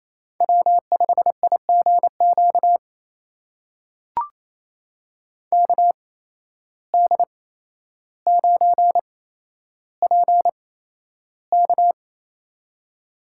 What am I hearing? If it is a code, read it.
Morse code, heard as a keyed tone, about 27 wpm, spaced as for 12 wpm, 700 Hz. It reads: W5IGQ KD9PK
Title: USA Call Signs - 5 Characters - 8x Character Spacing - 28wpm